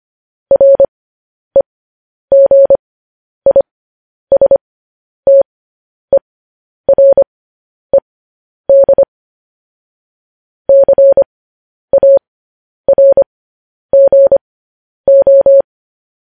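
Morse code, heard as a keyed tone, about 25 wpm, spaced as for 10 wpm, 550 Hz. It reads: REGISTERED CARGO
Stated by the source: Morse news headlines practice at 10.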